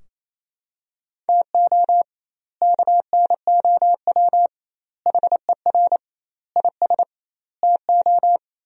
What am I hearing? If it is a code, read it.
Morse code, heard as a keyed tone, about 28 wpm, 700 Hz. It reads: TO KNOW HER IS TO
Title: 2x Word-Spacing - Sentences from Top 400 Words - 28wpm